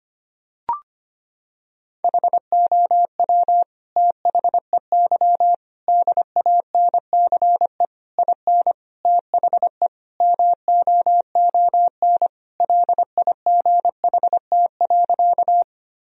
Morse code, heard as a keyed tone, about 25 wpm, 700 Hz. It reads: HOW THEY DANCE IN THE MOON LIGHT.
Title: Sentences from Top 1000 Words - Review Part 4 - 25wpm